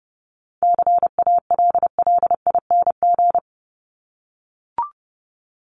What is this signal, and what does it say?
Morse code, heard as a keyed tone, about 30 wpm, 700 Hz. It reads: CALLING
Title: ING Suffix - Top 100 Words with Base Words 30wpm